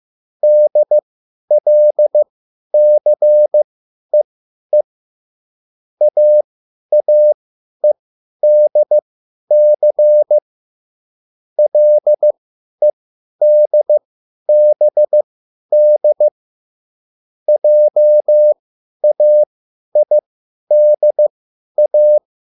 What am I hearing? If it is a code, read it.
Morse code, heard as a keyed tone, about 15 wpm, spaced as for 10 wpm, 600 Hz. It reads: DLCEE AAEDC LEDBD JAIDA
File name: QSL / CW lesson